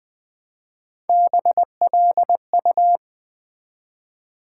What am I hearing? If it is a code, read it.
Morse code, heard as a keyed tone, about 20 wpm, 700 Hz. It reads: BLU
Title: Build A Word - 1 Letter At A Time - 3 to 5-Letter Words - 20wpm